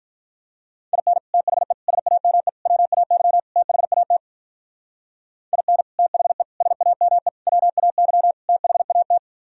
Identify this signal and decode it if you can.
Morse code, heard as a keyed tone, about 49 wpm, 700 Hz. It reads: IN THE SAME WAY THAT IN THE SAME WAY THAT